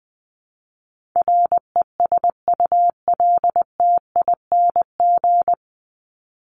Morse code, heard as a keyed tone, about 20 wpm, 700 Hz. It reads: RESULTING
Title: ING Suffix - Top 100 Words with Base Words 20wpm